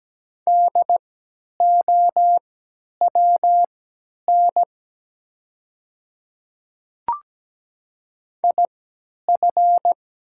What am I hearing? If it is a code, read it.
Morse code, heard as a keyed tone, about 17 wpm, spaced as for 10 wpm, 700 Hz. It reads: DOWN IF
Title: 3x Character Spacing - Top 100 Words - 17wpm